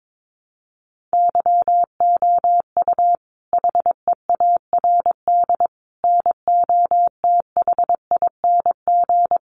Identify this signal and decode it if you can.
Morse code, heard as a keyed tone, about 22 wpm, 700 Hz. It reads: YOU HEARD NOTHING
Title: Sentences from Top 600 Words 22wpm